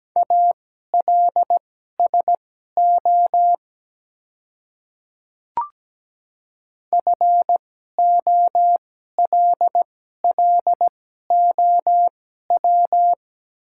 Morse code, heard as a keyed tone, about 17 wpm, spaced as for 12 wpm, 700 Hz. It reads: ALSO FOLLOW
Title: 2x Character Spacing - Top 200 Words - 17wpm